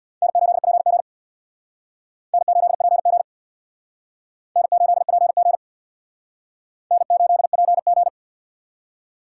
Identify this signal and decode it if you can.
Morse code, heard as a keyed tone, about 60 wpm, 700 Hz. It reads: N8PG N8PG N8PG N8PG
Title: Speed Racing - USA 1x2 Call Signs 50wpm